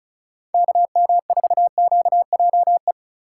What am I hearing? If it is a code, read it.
Morse code, heard as a keyed tone, about 36 wpm, 700 Hz. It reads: KM4QJE
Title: USA Call Signs - 6 Characters - 6x Character Spacing - 35wpm